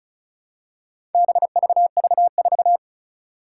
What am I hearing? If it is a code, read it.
Morse code, heard as a keyed tone, about 36 wpm, 700 Hz. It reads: BVV4
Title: Four Challenging Letter-Number Combinations 35wpm